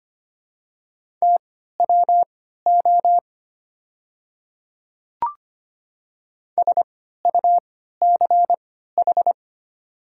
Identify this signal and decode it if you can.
Morse code, heard as a keyed tone, about 25 wpm, spaced as for 14 wpm, 700 Hz. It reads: TWO SUCH
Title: 3x Character Spacing - Top 200 Words - 25wpm